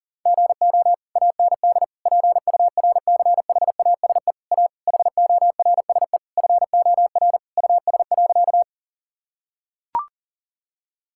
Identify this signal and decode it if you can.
Morse code, heard as a keyed tone, about 40 wpm, 700 Hz. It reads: GO AND PURCHASE A HORSE FOR US.
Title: Sentences from Top 1000 Words - Review Part 5 - 40wpm